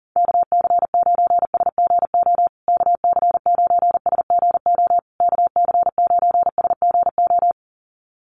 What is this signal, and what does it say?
Morse code, heard as a keyed tone, about 40 wpm, 700 Hz. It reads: KC9SGO KC9SGO KC9SGO
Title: USA Call Signs Repeated 3x 40wpm